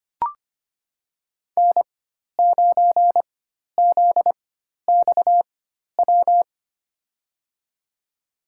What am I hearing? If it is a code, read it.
Morse code, heard as a keyed tone, about 25 wpm, spaced as for 12 wpm, 700 Hz. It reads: N9ZXW
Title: USA Call Signs - 5 Characters - 4x Character Spacing - 25wpm